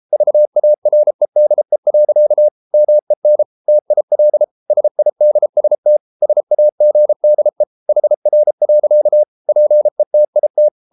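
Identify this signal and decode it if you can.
Morse code, heard as a keyed tone, about 33 wpm, 600 Hz. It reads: VAREDE. MEN TIL SIDST SAGDE HR. PETIT
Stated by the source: Morse code at 33 wpm, QRQ.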